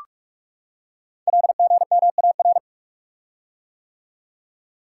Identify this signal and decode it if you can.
Morse code, heard as a keyed tone, about 44 wpm, 700 Hz. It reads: LGMAR
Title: Code Groups - 5 Letters - 45wpm